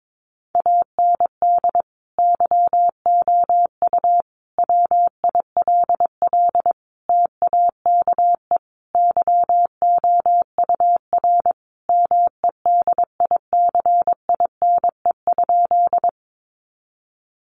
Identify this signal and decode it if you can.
Morse code, heard as a keyed tone, about 22 wpm, 700 Hz. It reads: AND YOU WILL TAKE YOUR MEDICINE?
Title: Sentences from Top 900 Words 22wpm